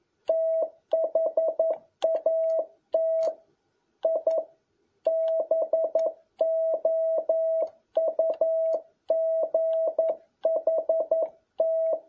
Morse code, heard as a keyed tone, about 11 wpm, 650 Hz. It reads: THAT I BOUGHT
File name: jh8cbh / 2021cw